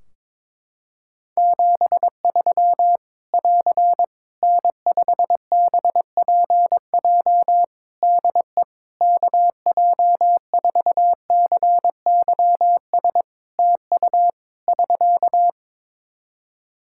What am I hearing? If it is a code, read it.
Morse code, heard as a keyed tone, about 22 wpm, 700 Hz. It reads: 73 <AR> N5BPJ DE KJ4CYS TU <SK>
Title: Practice QSO - Exchange 4 - Ending - 22wpm